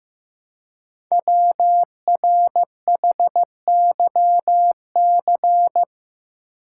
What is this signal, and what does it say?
Morse code, heard as a keyed tone, about 15 wpm, 700 Hz. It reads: WRHYC
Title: Code Groups - 5 Letters - 15wpm